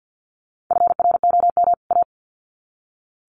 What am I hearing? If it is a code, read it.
Morse code, heard as a keyed tone, about 48 wpm, 700 Hz. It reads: FROM A
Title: Sets of 2 Words 50wpm